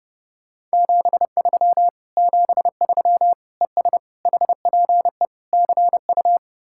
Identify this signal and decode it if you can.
Morse code, heard as a keyed tone, about 39 wpm, 700 Hz. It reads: 73 73 ES HPE CU